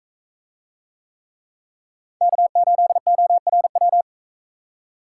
Morse code, heard as a keyed tone, about 44 wpm, 700 Hz. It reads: K8ORW